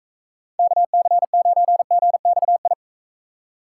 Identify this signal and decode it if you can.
Morse code, heard as a keyed tone, about 44 wpm, 700 Hz. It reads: KC9GXI